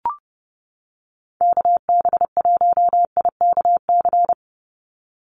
Morse code, heard as a keyed tone, about 30 wpm, 700 Hz. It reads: KB1IKC
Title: USA Call Signs 30wpm